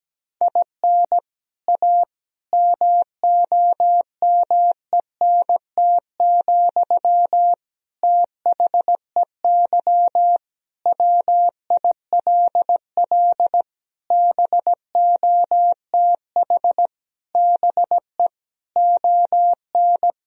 Morse code, heard as a keyed tone, about 17 wpm, 700 Hz. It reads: IN A MOMENT, THEY WILL BOTH BE ON
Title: Sentences from Top 1000 Words - Review Part 2 - 17wpm